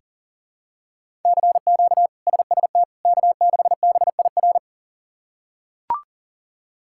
Morse code, heard as a keyed tone, about 37 wpm, 700 Hz. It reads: CQ SST K6BIR
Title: K1USN SST QSO Practice - 40wpm